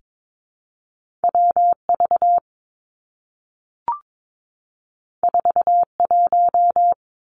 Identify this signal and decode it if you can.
Morse code, heard as a keyed tone, about 22 wpm, 700 Hz. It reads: W V 4 1
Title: Two Challenging Letter-Number Combinations 22wpm